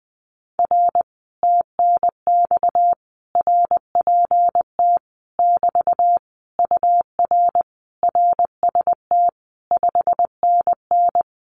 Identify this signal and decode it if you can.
Morse code, heard as a keyed tone, about 20 wpm, 700 Hz. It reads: R TNX RPT <BT> UR RST 5NN
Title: Common QSO Phrases Encore 20wpm